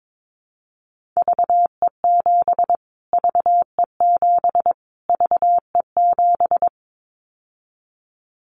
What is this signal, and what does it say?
Morse code, heard as a keyed tone, about 22 wpm, 700 Hz. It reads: VE7 VE7 VE7
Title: Callsign Prefixes for Top 25 DXCC - Repeated 3x - 22wpm